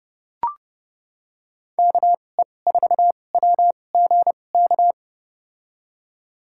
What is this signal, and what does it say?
Morse code, heard as a keyed tone, about 30 wpm, spaced as for 22 wpm, 700 Hz. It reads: KE4WGK
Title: USA Call Signs - 6 Characters - 2x Character Spacing - 30wpm